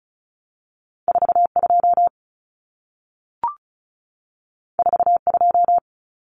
Morse code, heard as a keyed tone, about 36 wpm, 700 Hz. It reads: 42 42
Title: Numbers - 2-Digit 35wpm